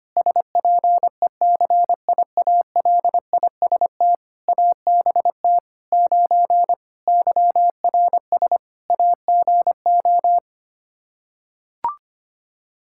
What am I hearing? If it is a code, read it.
Morse code, heard as a keyed tone, about 25 wpm, 700 Hz. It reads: SPECIALIST ABT 9 YRS AGO
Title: Practice QSO - Exchange 3 - More Info - 25wpm